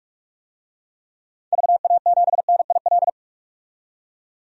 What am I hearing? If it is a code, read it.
Morse code, heard as a keyed tone, about 46 wpm, 700 Hz. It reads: VA7NIL
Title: DXCC-100 Call Signs - 6 Characters - 45wpm